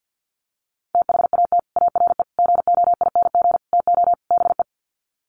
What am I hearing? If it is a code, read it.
Morse code, heard as a keyed tone, about 48 wpm, 700 Hz. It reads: THAT ARE GOING TO BE
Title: Sets of 5 Words Encore2 50wpm